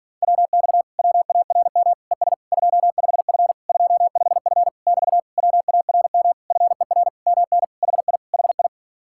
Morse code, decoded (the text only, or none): WX WARM ES 25F 25F <BT> WARM FER MN HI HI